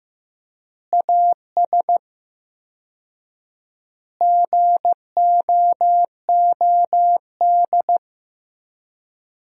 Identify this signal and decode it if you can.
Morse code, heard as a keyed tone, about 15 wpm, 700 Hz. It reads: AS GOOD
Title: 4x Word-Spacing - Sentences from Top 200 Words - 15wpm